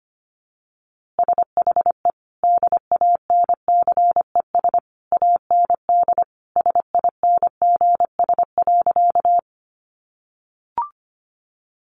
Morse code, heard as a keyed tone, about 25 wpm, 700 Hz. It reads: SHE DANCES AND SINGS.
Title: Easy Phrases 25wpm